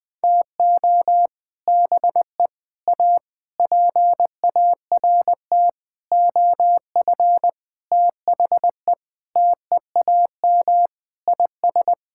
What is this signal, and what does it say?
Morse code, heard as a keyed tone, about 20 wpm, 700 Hz. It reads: TO BE A PART OF THE TEAM IS
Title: Sentences with 6-Word Phrases - Spoken Beginning - 20wpm